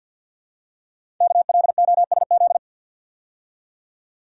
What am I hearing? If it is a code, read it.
Morse code, heard as a keyed tone, about 48 wpm, 690 Hz. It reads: KLOSZ
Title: Code Groups - 5 Letters - 50wpm